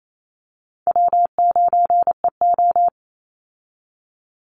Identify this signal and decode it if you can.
Morse code, heard as a keyed tone, about 28 wpm, 700 Hz. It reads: W9EO
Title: Call Signs Repeated 2x - 28wpm